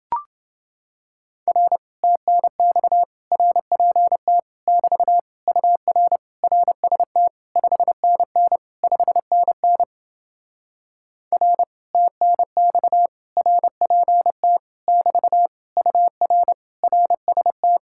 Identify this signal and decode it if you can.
Morse code, heard as a keyed tone, about 29 wpm, 700 Hz. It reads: R TNX RPT <BT> UR RST 5NN 5NN R TNX RPT <BT> UR RST